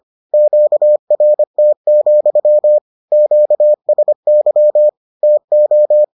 Morse code, heard as a keyed tone, about 25 wpm, 600 Hz. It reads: QRT, QSY TO